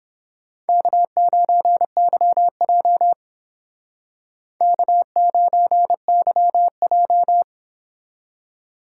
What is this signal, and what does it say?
Morse code, heard as a keyed tone, about 28 wpm, 700 Hz. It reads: K9YJ K9YJ